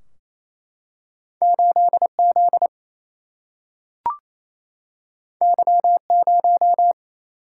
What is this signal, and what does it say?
Morse code, heard as a keyed tone, about 28 wpm, 700 Hz. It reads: 8 Z Y 0
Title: Single-Letter Single-Number 28wpm